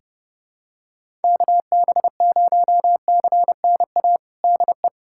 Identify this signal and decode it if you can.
Morse code, heard as a keyed tone, about 32 wpm, 700 Hz. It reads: KB0CNA DE
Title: USA Callsign DE Callsign - 6 Characters - 30wpm